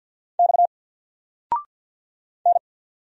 Morse code, heard as a keyed tone, about 50 wpm, 700 Hz. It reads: X N
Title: Single Letter-Number 50wpm